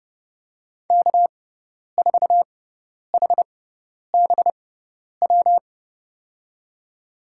Morse code, heard as a keyed tone, about 30 wpm, spaced as for 10 wpm, 700 Hz. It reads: K4HBW
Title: USA Call Signs - 5 Characters - 6x Character Spacing - 30wpm